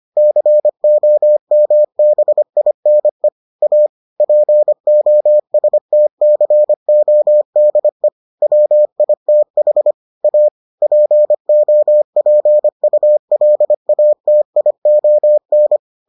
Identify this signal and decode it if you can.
Morse code, heard as a keyed tone, about 25 wpm, 600 Hz. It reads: COMBINE A POSTCODE WITH A POPULATION